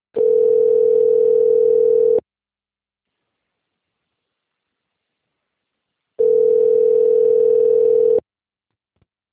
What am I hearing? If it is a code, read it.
Ringback tone (North America), 440 + 480 Hz.
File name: rg-1600-unknown- - - .1153